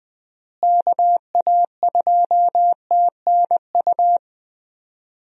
Morse code, heard as a keyed tone, about 20 wpm, 700 Hz. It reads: KA2TNU
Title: USA Call Signs - 6 Characters - Repeated 2x - 20wpm